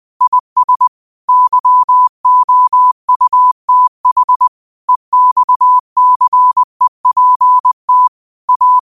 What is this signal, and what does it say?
Morse code, heard as a keyed tone, about 20 wpm, 1000 Hz. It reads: IS YOUTH EXCEPT A
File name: qotdfiles / 20wpm